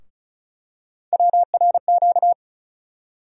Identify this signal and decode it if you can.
Morse code, heard as a keyed tone, about 36 wpm, 700 Hz. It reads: WRQ
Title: USA Call Sign Postfixes - 3 Letters 35wpm